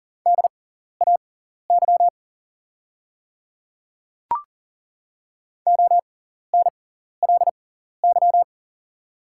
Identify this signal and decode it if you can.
Morse code, heard as a keyed tone, about 40 wpm, spaced as for 14 wpm, 700 Hz. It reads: DAY ONLY
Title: 6x Character Spacing - Top 300 Words - 40wpm